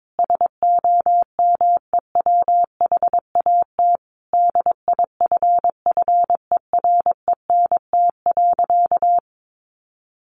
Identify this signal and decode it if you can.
Morse code, heard as a keyed tone, about 22 wpm, 700 Hz. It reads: SOMEWHAT DIFFERENT.
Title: Sentences from Top 1300 Words 22wpm